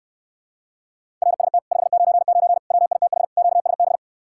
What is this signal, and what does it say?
Morse code, heard as a keyed tone, about 69 wpm, 700 Hz. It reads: RST 599 WITH QSB